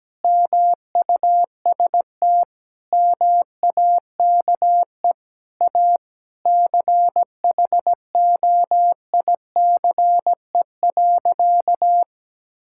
Morse code, heard as a keyed tone, about 17 wpm, 700 Hz. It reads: MUST MAKE A CHOICE.
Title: Sentences from Top 1000 Words - Review Part 6 - 17wpm